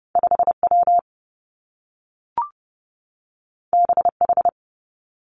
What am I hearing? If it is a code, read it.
Morse code, heard as a keyed tone, about 30 wpm, 700 Hz. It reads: 5 W B H
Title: Two Challenging Letter-Number Combinations 30wpm